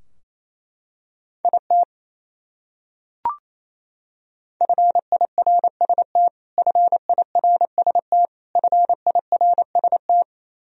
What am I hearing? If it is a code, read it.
Morse code, heard as a keyed tone, about 28 wpm, 700 Hz. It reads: IT FIRST FIRST FIRST
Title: Top 100 Words Repeated 28wpm